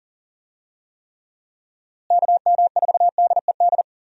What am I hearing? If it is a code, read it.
Morse code, heard as a keyed tone, about 40 wpm, 700 Hz. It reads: KM4DED
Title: USA Call Signs - 6 Characters - Repeated 3x - 40wpm